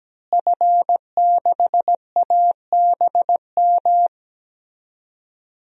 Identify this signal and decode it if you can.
Morse code, heard as a keyed tone, about 17 wpm, 700 Hz. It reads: F6ABM
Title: DXCC-100 Call Signs - 5 Characters - Repeated 3x - 17wpm